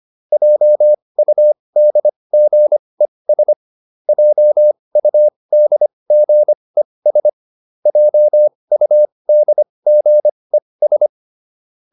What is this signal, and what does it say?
Morse code, heard as a keyed tone, about 25 wpm, spaced as for 20 wpm, 600 Hz. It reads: JUDGES JUDGES JUDGES